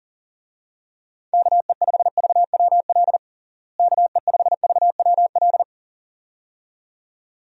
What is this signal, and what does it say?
Morse code, heard as a keyed tone, about 40 wpm, 700 Hz. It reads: KE5VWL KE5VWL
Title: USA Call Signs - 6 Characters - Repeated 2x - 40wpm